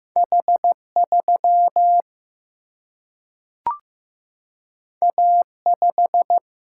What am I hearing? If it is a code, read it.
Morse code, heard as a keyed tone, about 15 wpm, 700 Hz. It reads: H 3 A 5
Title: Single-Letter Single-Number 15wpm